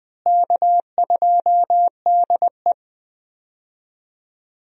Morse code, heard as a keyed tone, about 20 wpm, 700 Hz. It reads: K2DE
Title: USA Call Signs - 4 Characters - Repeated 2x - 20wpm